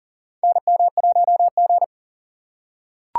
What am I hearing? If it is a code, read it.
Morse code, heard as a keyed tone, about 39 wpm, 700 Hz. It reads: NM1G
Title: USA Call Signs - 4 Characters - 4x Character Spacing - 40wpm